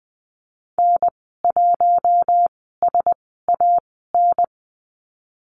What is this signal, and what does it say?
Morse code, heard as a keyed tone, about 19 wpm, spaced as for 14 wpm, 700 Hz. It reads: N1SAN
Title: 048 Lesson 9g - Number 4 - Call Signs 20wpm